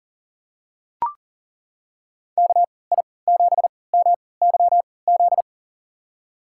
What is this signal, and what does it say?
Morse code, heard as a keyed tone, about 40 wpm, spaced as for 23 wpm, 700 Hz. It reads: KI7MYZ
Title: USA Call Signs - 3x Character Spacing - 40wpm